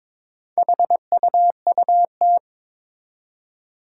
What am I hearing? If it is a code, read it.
Morse code, heard as a keyed tone, about 22 wpm, 700 Hz. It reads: HUUT
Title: Four Challenging Letter-Number Combinations 22wpm